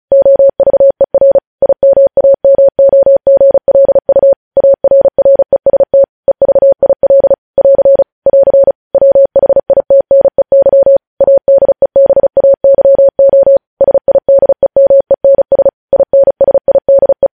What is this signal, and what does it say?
Morse code, heard as a keyed tone, about 36 wpm, 550 Hz. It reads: OVER IMAMOGLU ARREST EVIL <AR> <AR> WHITNEY ADEBAYO SIDEMENS INSIDE
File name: podcast / News headlines-1x-35x35